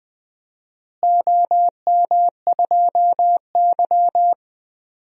Morse code, heard as a keyed tone, about 20 wpm, 700 Hz. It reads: OM2Y
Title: DXCC-100 Call Signs - 4 Characters - Repeated 2x - 20wpm